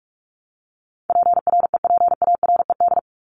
Morse code, heard as a keyed tone, about 44 wpm, 700 Hz. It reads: PREPARED
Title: ED Suffix - Top 100 Words with Base Words 45wpm